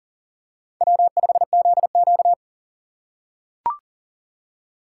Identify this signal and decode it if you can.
Morse code, heard as a keyed tone, about 40 wpm, 700 Hz. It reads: W5ZQ
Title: USA Call Signs - 4 Characters - 4x Character Spacing - 40wpm